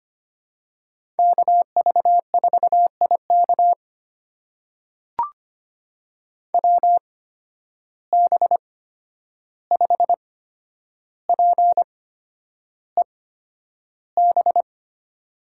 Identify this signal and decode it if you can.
Morse code, heard as a keyed tone, about 25 wpm, spaced as for 10 wpm, 700 Hz. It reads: KV4IK WB5PEB
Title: USA Call Signs - 8x Character Spacing - 25wpm